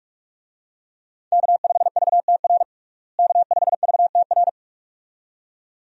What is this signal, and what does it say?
Morse code, heard as a keyed tone, about 41 wpm, 700 Hz. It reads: K5VTR K5VTR
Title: USA Call Signs - 5 Characters - Repeated 2x - 45wpm